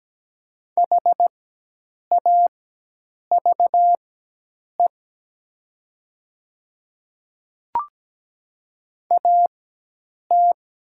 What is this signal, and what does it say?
Morse code, heard as a keyed tone, about 17 wpm, spaced as for 8 wpm, 700 Hz. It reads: HAVE AT
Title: 4x Character Spacing - Top 300 Words - 17wpm